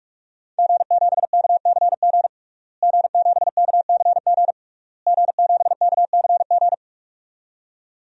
Morse code, heard as a keyed tone, about 46 wpm, 700 Hz. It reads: G7KCG G7KCG G7KCG
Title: DXCC-100 Call Signs - 5 Characters - Repeated 3x - 45wpm